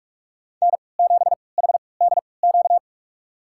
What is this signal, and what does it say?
Morse code, heard as a keyed tone, about 46 wpm, spaced as for 25 wpm, 700 Hz. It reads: N7HDQ
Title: USA Call Signs - 5 Characters - 3x Character Spacing - 45wpm